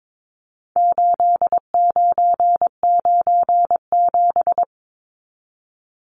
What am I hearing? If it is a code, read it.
Morse code, heard as a keyed tone, about 22 wpm, 700 Hz. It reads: 8997
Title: Numbers - 4-Digit 22wpm